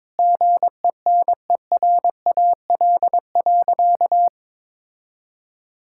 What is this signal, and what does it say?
Morse code, heard as a keyed tone, about 22 wpm, 700 Hz. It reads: GENERAL.
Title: Sentences from Top 1600 Words 22wpm